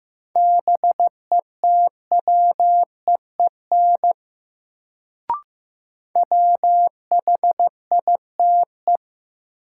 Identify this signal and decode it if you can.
Morse code, heard as a keyed tone, about 15 wpm, 700 Hz. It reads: BETWEEN WHITE
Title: Sets of 2 Words - New Word Drill - 15wpm